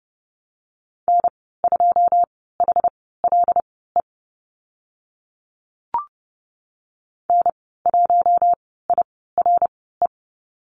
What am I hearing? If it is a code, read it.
Morse code, heard as a keyed tone, about 28 wpm, spaced as for 17 wpm, 700 Hz. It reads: N2HLE N1IRE
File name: Lesson 15g - Number 5 - Call Signs 30wpm